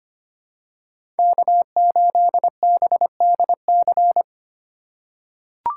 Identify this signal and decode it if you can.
Morse code, heard as a keyed tone, about 25 wpm, 700 Hz. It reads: K8BDC
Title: USA Call Signs - 5 Characters - 3x Character Spacing - 25wpm